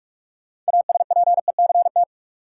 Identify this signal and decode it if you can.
Morse code, heard as a keyed tone, about 46 wpm, 700 Hz. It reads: ASPECT
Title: Sentences from Top 1300 Words - New Word Drill Alt - 45wpm